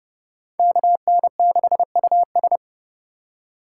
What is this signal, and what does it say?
Morse code, heard as a keyed tone, about 29 wpm, 700 Hz. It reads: KN6US